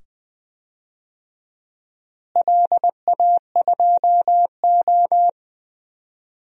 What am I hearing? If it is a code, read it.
Morse code, heard as a keyed tone, about 20 wpm, 700 Hz. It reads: LA2O
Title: DXCC-100 Call Signs - 4 Characters - Repeated 2x - 20wpm